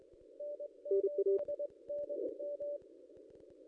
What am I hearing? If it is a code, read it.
Morse code, heard as a keyed tone, about 21 wpm, 580 Hz. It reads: N8NM
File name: contests / ss-cw-k5tr-op-n5rz -0500